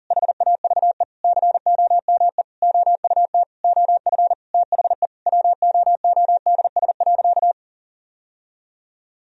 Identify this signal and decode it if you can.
Morse code, heard as a keyed tone, about 40 wpm, 700 Hz. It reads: HAVE COME OUT OF THE WOODS.
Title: Sentences from Top 1000 Words - Review Part 2 - 40wpm